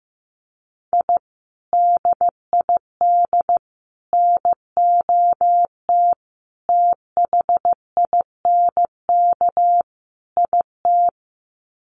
Morse code, heard as a keyed tone, about 15 wpm, 700 Hz. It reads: I DID NOT THINK IT
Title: Sets of 5 Words Encore2 15wpm